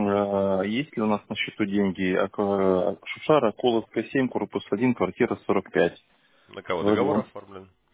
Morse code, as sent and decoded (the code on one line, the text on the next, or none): none